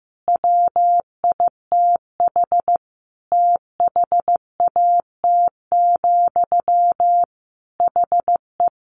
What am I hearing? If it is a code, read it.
Morse code, heard as a keyed tone, about 15 wpm, 700 Hz. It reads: WITH THAT, HE
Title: Sentences from Top 1000 Words 15wpm